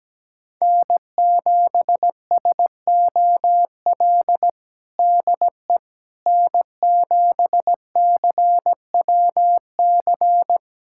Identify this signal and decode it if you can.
Morse code, heard as a keyed tone, about 17 wpm, 700 Hz. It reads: N7SOL DE N7CWC
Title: USA Callsign DE Callsign - 5 Characters - 17wpm